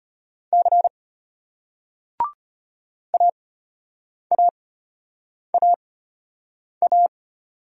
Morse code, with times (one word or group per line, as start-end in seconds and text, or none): 0.5-0.9 C
3.1-7.1 AAAA